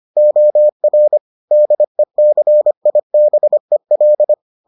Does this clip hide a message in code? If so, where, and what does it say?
Yes: Morse code from 0.2 to 4.3 s, heard as a keyed tone, about 25 wpm, 600 Hz. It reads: OR DECIBEL